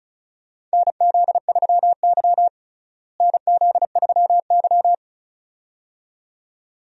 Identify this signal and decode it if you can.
Morse code, heard as a keyed tone, about 36 wpm, 700 Hz. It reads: NZ3Y NZ3Y
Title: USA Call Signs - 4 Characters - Repeated 2x - 35wpm